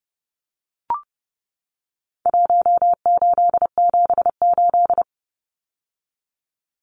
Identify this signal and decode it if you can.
Morse code, heard as a keyed tone, about 30 wpm, 700 Hz. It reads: 1878
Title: Numbers - Contest Style - 4-Digit 30wpm